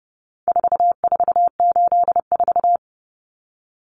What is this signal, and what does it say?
Morse code, heard as a keyed tone, about 30 wpm, 700 Hz. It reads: 4484
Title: Common Cut Numbers - 4-Digit 30wpm